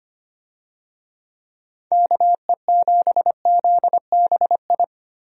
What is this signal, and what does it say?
Morse code, heard as a keyed tone, about 25 wpm, 700 Hz. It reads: KE7ZBI
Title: USA Call Signs - 6 Characters - Repeated 2x - 25wpm